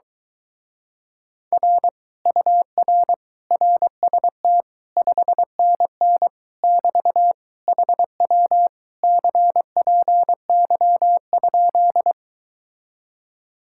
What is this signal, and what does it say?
Morse code, heard as a keyed tone, about 23 wpm, 700 Hz. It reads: R UR RST 5NN <BT> HW CPY?